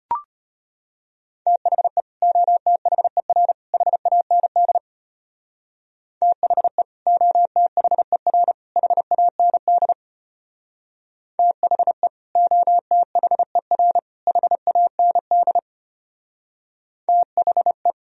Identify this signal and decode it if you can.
Morse code, heard as a keyed tone, about 32 wpm, 700 Hz. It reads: THE OTHER HAND THE OTHER HAND THE OTHER HAND THE